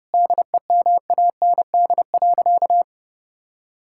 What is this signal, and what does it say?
Morse code, heard as a keyed tone, about 30 wpm, 700 Hz. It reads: DEMAND.
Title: Sentences from Top 1000 Words - Review Part 5 - 30wpm